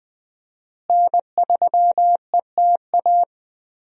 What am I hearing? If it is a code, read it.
Morse code, heard as a keyed tone, about 20 wpm, 700 Hz. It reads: N3ETA
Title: USA Call Signs - 5 Characters - Repeated 2x - 20wpm